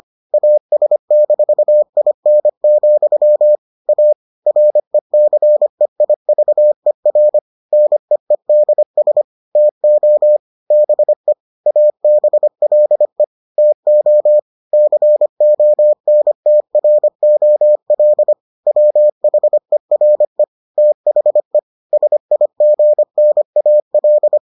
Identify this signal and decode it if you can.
Morse code, heard as a keyed tone, about 25 wpm, 600 Hz. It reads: AS-IN, A RECEIVER NEEDS TO BE ABLE TO CONTROL WHERE THE SIGNAL